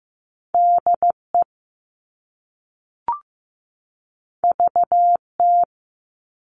Morse code, heard as a keyed tone, about 15 wpm, 700 Hz. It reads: DE VT
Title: US States 15wpm